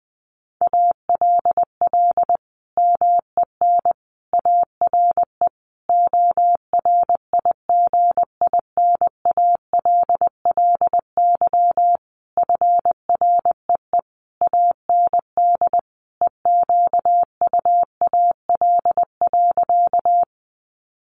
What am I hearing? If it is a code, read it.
Morse code, heard as a keyed tone, about 20 wpm, 700 Hz. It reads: ALL MEN ARE ORIGINALLY FREE AND EQUAL.
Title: Sentences from Top 1500 Words 20wpm